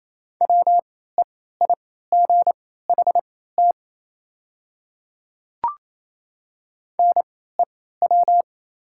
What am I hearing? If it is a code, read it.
Morse code, heard as a keyed tone, about 28 wpm, spaced as for 16 wpm, 700 Hz. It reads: WEIGHT NEW
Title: 3x Character Spacing - Top 500 Words - 28wpm